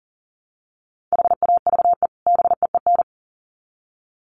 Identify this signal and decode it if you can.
Morse code, heard as a keyed tone, about 40 wpm, 700 Hz. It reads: HAVE BEEN
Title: Sets of 2 Words 40wpm